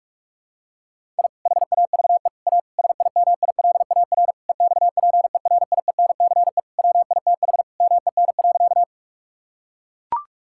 Morse code, heard as a keyed tone, about 41 wpm, 700 Hz. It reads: I HAVE A SIMILAR EXPERIENCE WITH MEN.
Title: Sentences from Top 1000 Words - Review Part 5 - 45wpm